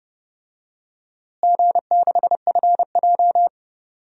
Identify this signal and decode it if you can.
Morse code, heard as a keyed tone, about 30 wpm, 700 Hz. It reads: G6FJ
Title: DXCC-100 Call Signs - 4 Characters - Repeated 3x - 30wpm